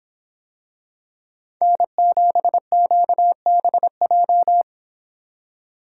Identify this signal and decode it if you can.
Morse code, heard as a keyed tone, about 24 wpm, 700 Hz. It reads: N7QBJ